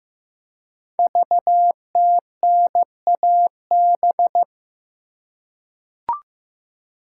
Morse code, heard as a keyed tone, about 15 wpm, 700 Hz. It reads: VTNAB
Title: Code Groups - 5 Letters - 15wpm